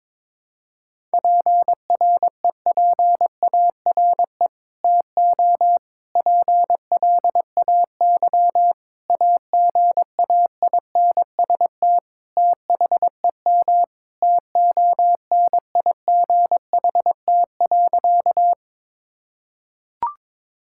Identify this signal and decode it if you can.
Morse code, heard as a keyed tone, about 22 wpm, 700 Hz. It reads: PREPARE TO PLAY AGAINST THEM TONIGHT.
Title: Sentences from Top 1000 Words - Review Part 6 - 22wpm